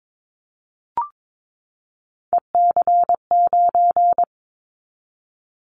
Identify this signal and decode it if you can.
Morse code, heard as a keyed tone, about 22 wpm, 700 Hz. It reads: EC9
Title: Callsign Prefixes for Top 100 DXCC - 22wpm